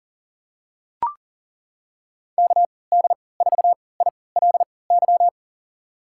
Morse code, heard as a keyed tone, about 40 wpm, spaced as for 23 wpm, 700 Hz. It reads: KD4ILY
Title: USA Call Signs - 6 Characters - 3x Character Spacing - 40wpm